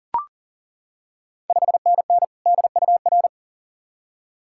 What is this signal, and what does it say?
Morse code, heard as a keyed tone, about 40 wpm, 690 Hz. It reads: 5NN DUR
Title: Ontario QSO Party Exchange 40wpm